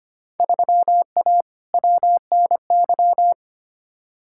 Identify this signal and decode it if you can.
Morse code, heard as a keyed tone, about 25 wpm, 700 Hz. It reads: 3A WNY
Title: Field Day Exchanges - 25wpm